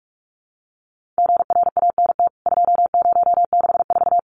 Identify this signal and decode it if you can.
Morse code, heard as a keyed tone, about 44 wpm, 700 Hz. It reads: GRANT 2064
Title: CWT Exchange 45wpm